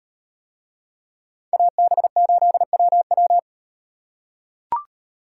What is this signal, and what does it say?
Morse code, heard as a keyed tone, about 37 wpm, 700 Hz. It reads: AB8WW